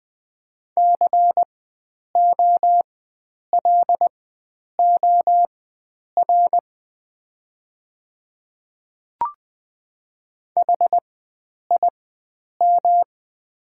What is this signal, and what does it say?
Morse code, heard as a keyed tone, about 20 wpm, spaced as for 9 wpm, 700 Hz. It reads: COLOR HIM
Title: 4x Character Spacing - Top 500 Words - 20wpm